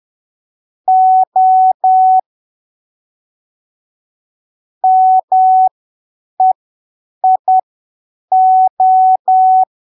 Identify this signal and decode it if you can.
Morse code, heard as a keyed tone, about 10 wpm, spaced as for 7 wpm, 750 Hz. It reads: O MEIO
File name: arquivos / texto / texto 10 10ppm